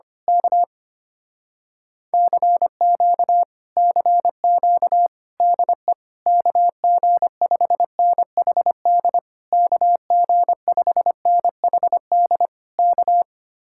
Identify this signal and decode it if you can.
Morse code, heard as a keyed tone, about 24 wpm, 700 Hz. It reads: K CQ CQ DE KG5NHD KG5NHD K
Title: Speed-Racing Common QSO Phrases - 25wpm